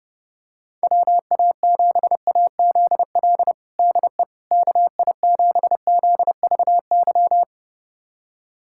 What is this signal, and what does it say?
Morse code, heard as a keyed tone, about 30 wpm, 700 Hz. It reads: WA7AZL DE KI7ZVY
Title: USA Callsign DE Callsign - 6 Characters - 30wpm